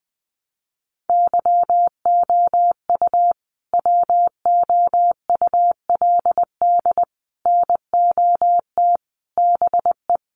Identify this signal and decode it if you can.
Morse code, heard as a keyed tone, about 20 wpm, 700 Hz. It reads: YOU WOULD NOT BE
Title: Sets of 5 Words Encore2 20wpm